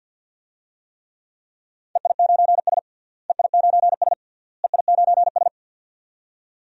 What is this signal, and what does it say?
Morse code, heard as a keyed tone, about 48 wpm, 700 Hz. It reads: EI9S EI9S EI9S
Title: DXCC-100 Call Signs - 4 Characters - Repeated 3x - 50wpm